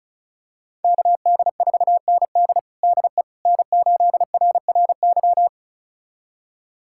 Morse code, heard as a keyed tone, about 36 wpm, 700 Hz. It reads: KD4ND DE N8RRY
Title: USA Callsign DE Callsign - 5 Characters - 35wpm